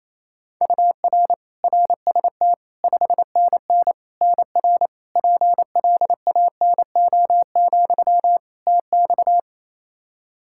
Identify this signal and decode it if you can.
Morse code, heard as a keyed tone, about 28 wpm, 700 Hz. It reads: UR RST 5NN NR PLANO, TX
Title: Practice QSO - Exchange 1 - Beginning - 28wpm